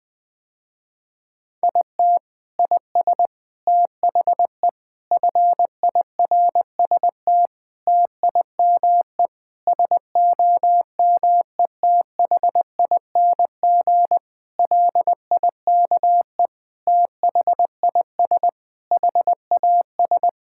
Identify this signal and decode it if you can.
Morse code, heard as a keyed tone, about 20 wpm, 700 Hz. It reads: IT IS THE FIRST TIME SOMETHING LIKE THIS HAS
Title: Sentences with 8-Word Phrases - Spoken Beginning - 20wpm